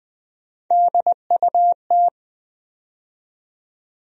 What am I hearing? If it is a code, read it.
Morse code, heard as a keyed tone, about 20 wpm, 700 Hz. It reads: DUT
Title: USA Call Sign Postfixes - 3 Letters 20wpm